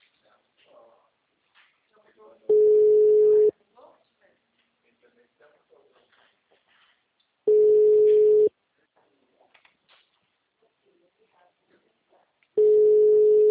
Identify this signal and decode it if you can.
Ringback tone (Europe, 425 Hz), with speech.